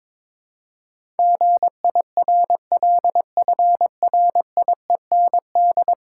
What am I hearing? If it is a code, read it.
Morse code, heard as a keyed tone, about 23 wpm, 700 Hz. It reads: GIRLFRIEND
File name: practice / 22wpm / 10 Letter Words - First 3 Letters Given - 22wpm